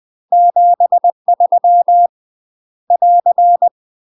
Morse code, heard as a keyed tone, about 20 wpm, 700 Hz. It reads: 73 <AR>